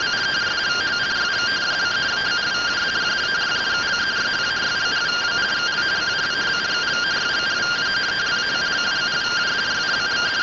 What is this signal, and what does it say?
Two-tone teleprinter signal, about 170 Hz shift.